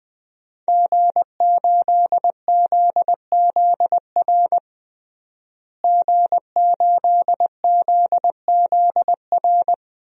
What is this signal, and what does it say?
Morse code, heard as a keyed tone, about 20 wpm, 700 Hz. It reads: G8ZZR G8ZZR
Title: DXCC-100 Call Signs - 5 Characters - Repeated 3x - 20wpm